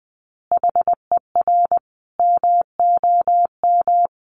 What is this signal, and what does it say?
Morse code, heard as a keyed tone, about 20 wpm, 700 Hz. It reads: HER MOM
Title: Easy Phrases 20wpm